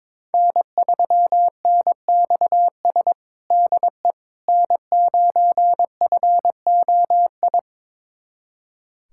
Morse code, heard as a keyed tone, about 22 wpm, 700 Hz. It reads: N3NXS DE N9FOI
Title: USA Callsign DE Callsign - 5 Characters - 22wpm